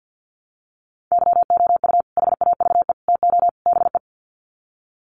Morse code, heard as a keyed tone, about 50 wpm, 700 Hz. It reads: YOU HAVE TO BE
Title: Sets of 4 Words 50wpm